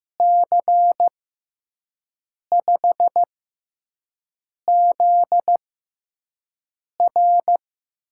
Morse code, heard as a keyed tone, about 15 wpm, spaced as for 5 wpm, 700 Hz. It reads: C5ZR
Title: USA Call Signs - 6x Character Spacing - 15wpm